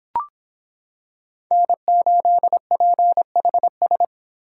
Morse code, heard as a keyed tone, about 26 wpm, 700 Hz. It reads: N8PHS